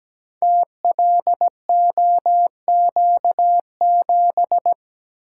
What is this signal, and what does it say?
Morse code, heard as a keyed tone, about 17 wpm, 700 Hz. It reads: TLOQ7
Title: Code Groups - 5 Letter:Numbers - 17wpm